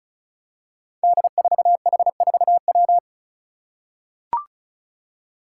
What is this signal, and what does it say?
Morse code, heard as a keyed tone, about 34 wpm, 700 Hz. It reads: D4H4W
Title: Five Challenging Letter-Number Combinations 35wpm